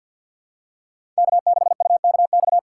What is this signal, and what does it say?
Morse code, heard as a keyed tone, about 52 wpm, 700 Hz. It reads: K6UKX
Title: USA Call Signs - 5 Characters - 3x Character Spacing - 50wpm